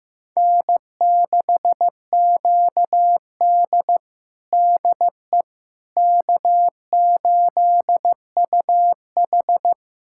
Morse code, heard as a keyed tone, about 15 wpm, 700 Hz. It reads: N6QD DE K8UH
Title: USA Callsign DE Callsign - 4 Characters - 15wpm